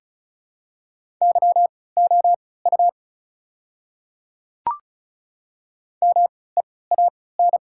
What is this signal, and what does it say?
Morse code, heard as a keyed tone, about 36 wpm, spaced as for 20 wpm, 700 Hz. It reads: YOU MEAN
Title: 3x Character Spacing - Top 200 Words - 35wpm